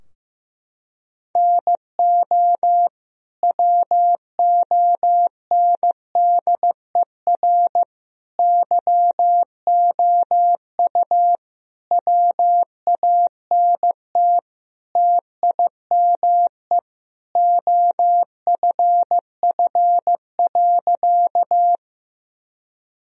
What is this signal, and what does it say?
Morse code, heard as a keyed tone, about 15 wpm, 700 Hz. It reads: NO WONDER YOU WANT TIME OFF.
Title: Sentences from Top 1000 Words - Review Part 3 - 15wpm